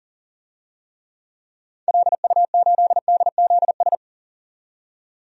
Morse code, heard as a keyed tone, about 38 wpm, 700 Hz. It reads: LU8DZS
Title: DXCC-100 Call Signs - 6 Characters - 40wpm